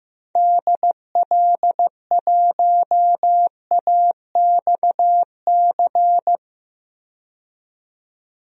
Morse code, heard as a keyed tone, about 15 wpm, 700 Hz. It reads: DL1AXC
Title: DXCC-100 Call Signs - 6 Characters - 15wpm